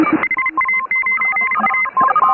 Morse code, heard as a keyed tone, about 34 wpm, 2100 Hz, with noise. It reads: HA9A
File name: AUDIO2018 / HA3FMR CW